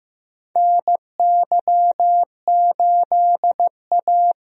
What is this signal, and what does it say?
Morse code, heard as a keyed tone, about 15 wpm, 700 Hz. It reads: NY8A
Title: USA Call Signs - 4 Characters - Repeated 3x - 15wpm